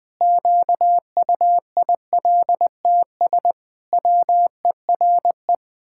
Morse code, heard as a keyed tone, about 20 wpm, 700 Hz. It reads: QUILTS WERE
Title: Pangram Challenge v2 - 20wpm